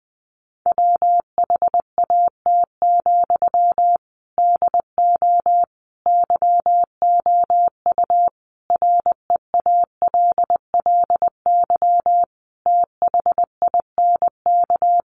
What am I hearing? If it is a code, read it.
Morse code, heard as a keyed tone, about 20 wpm, 700 Hz. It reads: WHAT, DO YOU REALLY THINK
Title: Sentences from Top 600 Words 20wpm